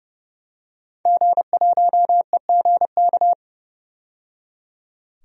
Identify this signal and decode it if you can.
Morse code, heard as a keyed tone, about 30 wpm, 700 Hz. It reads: G1EGK
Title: DXCC-100 Call Signs - 5 Characters - 30wpm